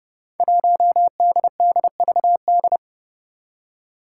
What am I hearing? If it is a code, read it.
Morse code, heard as a keyed tone, about 30 wpm, 700 Hz. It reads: 1DDVD
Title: Five Challenging Letter-Number Combinations 30wpm